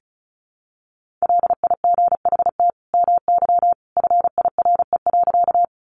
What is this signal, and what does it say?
Morse code, heard as a keyed tone, about 34 wpm, 700 Hz. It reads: LIGHT MY FIRE.
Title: Sentences from Top 400 Words 35wpm